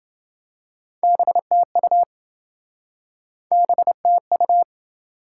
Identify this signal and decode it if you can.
Morse code, heard as a keyed tone, about 27 wpm, 700 Hz. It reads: BTU BTU